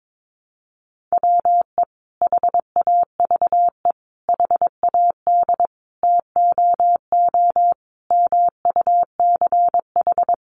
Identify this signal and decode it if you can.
Morse code, heard as a keyed tone, about 22 wpm, 700 Hz. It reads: WE HAVE HAD TOO MUCH
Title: Sentences from Top 700 Words 22wpm